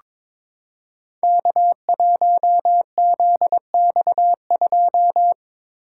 Morse code, heard as a keyed tone, about 22 wpm, 700 Hz. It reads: K1ZX2
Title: Code Groups - 5 Letter:Numbers - 22wpm